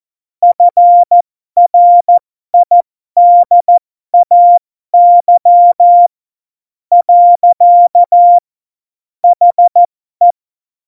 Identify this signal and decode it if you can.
Morse code, heard as a keyed tone, about 14 wpm, spaced as for 11 wpm, 700 Hz. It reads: FRIDAY. HE
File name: Morsekurs / Witze